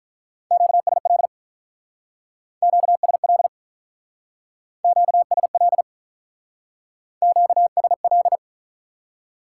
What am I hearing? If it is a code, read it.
Morse code, heard as a keyed tone, about 47 wpm, 700 Hz. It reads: QSL QSL QSL QSL